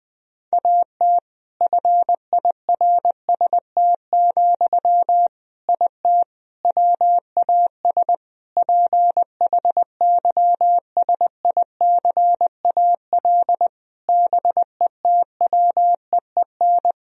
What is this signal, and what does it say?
Morse code, heard as a keyed tone, about 20 wpm, 700 Hz. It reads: AT FIRST, IT WAS PHYSICAL BETWEEN
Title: Sentences from Top 1000 Words - Review Part 4 - 20wpm